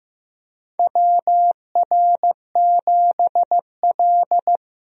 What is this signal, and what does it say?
Morse code, heard as a keyed tone, about 15 wpm, 700 Hz. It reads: WR7L
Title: USA Call Signs - 4 Characters - Repeated 3x - 15wpm